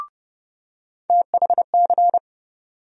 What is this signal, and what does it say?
Morse code, heard as a keyed tone, about 30 wpm, 700 Hz. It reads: THC
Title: USA Call Sign Postfixes - 3 Letters 30wpm